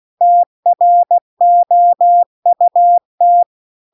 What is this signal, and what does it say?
Morse code, heard as a keyed tone, about 16 wpm, 700 Hz. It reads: TROUT